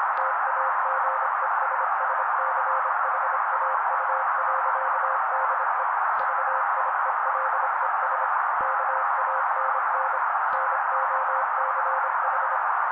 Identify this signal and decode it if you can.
Morse code, heard as a keyed tone, about 25 wpm, 530 Hz. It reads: TAGESSCHAU.DE VIELES KANN NOCH
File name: daily / 25wpm noise